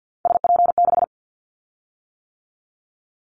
Morse code, heard as a keyed tone, about 50 wpm, 700 Hz. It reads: SP6
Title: Callsign Prefixes for Top 25 DXCC - 50wpm